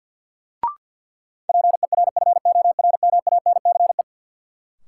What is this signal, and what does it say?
Morse code, heard as a keyed tone, about 52 wpm, 700 Hz. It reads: PERFORMANCE
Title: Sentences from Top 1000 Words - New Word Drill - 50wpm